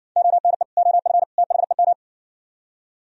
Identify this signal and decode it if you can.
Morse code, heard as a keyed tone, about 75 wpm, 700 Hz. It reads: ONE OF THEM